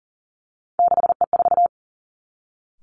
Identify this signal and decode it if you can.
Morse code, heard as a keyed tone, about 42 wpm, 700 Hz. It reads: 6E4
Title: Three Challenging Letter-Number Combinations 40wpm